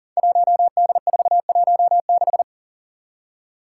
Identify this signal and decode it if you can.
Morse code, heard as a keyed tone, about 40 wpm, 690 Hz. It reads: 1D416
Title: Five Challenging Letter-Number Combinations 40wpm